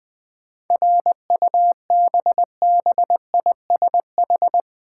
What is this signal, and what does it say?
Morse code, heard as a keyed tone, about 20 wpm, 700 Hz. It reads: RUBBISH
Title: 7 Letter Words - First 3 Letters Given - 20wpm